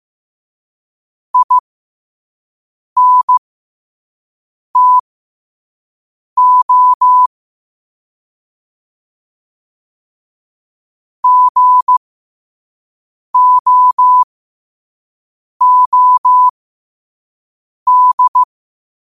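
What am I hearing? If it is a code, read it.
Morse code, heard as a keyed tone, about 15 wpm, spaced as for 5 wpm, 1000 Hz. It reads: INTO GOOD
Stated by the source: Quote-of-the-day Morse code practice at 5 wpm.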